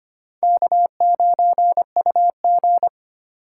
Morse code, heard as a keyed tone, about 25 wpm, 700 Hz. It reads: K9UG
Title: USA Call Signs - 2x Character Spacing - 25wpm